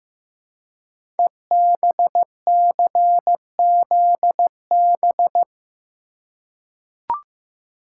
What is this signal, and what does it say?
Morse code, heard as a keyed tone, about 15 wpm, 700 Hz. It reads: EBCZB
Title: Code Groups - 5 Letters - 15wpm